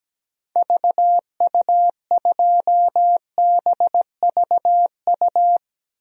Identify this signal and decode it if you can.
Morse code, heard as a keyed tone, about 17 wpm, 700 Hz. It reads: VU2BVU
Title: DXCC-100 Call Signs - 6 Characters - Repeated 3x - 17wpm